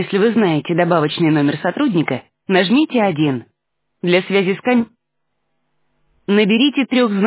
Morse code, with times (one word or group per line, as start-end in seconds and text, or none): none